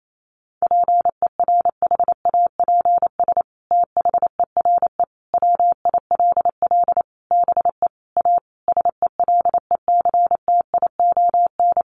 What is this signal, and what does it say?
Morse code, heard as a keyed tone, about 28 wpm, 700 Hz. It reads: PERHAPS THERE WILL BE A SELECTION
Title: Sentences from Top 1400 Words 28wpm